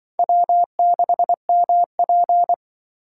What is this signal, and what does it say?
Morse code, heard as a keyed tone, about 24 wpm, 700 Hz. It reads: W6MP
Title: Speed Racing - USA 1x2 Call Signs 20wpm